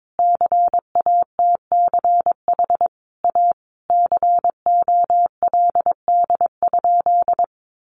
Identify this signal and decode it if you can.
Morse code, heard as a keyed tone, about 22 wpm, 700 Hz. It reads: CATCH A COLD?
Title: Sentences from Top 1500 Words 22wpm